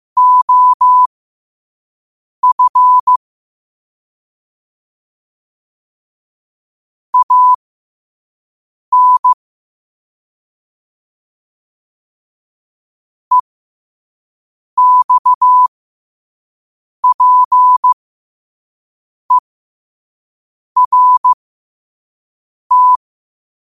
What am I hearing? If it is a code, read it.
Morse code, heard as a keyed tone, about 15 wpm, spaced as for 5 wpm, 1000 Hz. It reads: OF AN EXPERT
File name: 5wpm